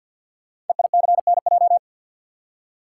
Morse code, heard as a keyed tone, about 48 wpm, 700 Hz. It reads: EICNJ
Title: Code Groups - 5 Letters - 50wpm